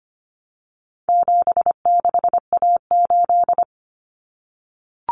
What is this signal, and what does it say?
Morse code, heard as a keyed tone, about 25 wpm, 700 Hz. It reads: 76A8
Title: Common Cut Numbers - 4-Digit 25wpm